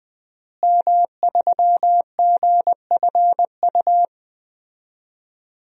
Morse code, heard as a keyed tone, about 21 wpm, 700 Hz. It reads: M3GFU
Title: DXCC-100 Call Signs - 5 Characters - Repeated 2x - 20wpm